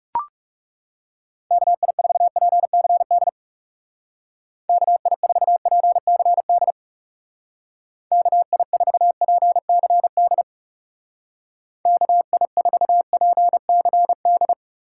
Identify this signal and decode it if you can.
Morse code, heard as a keyed tone, about 40 wpm, 700 Hz. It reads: KI4PCD KI4PCD KI4PCD KI4PCD